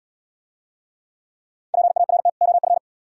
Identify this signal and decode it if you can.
Morse code, heard as a keyed tone, about 71 wpm, 700 Hz. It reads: PART OF